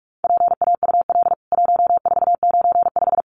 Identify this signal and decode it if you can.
Morse code, heard as a keyed tone, about 44 wpm, 700 Hz. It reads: PAUL 1495
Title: CWT Exchange 45wpm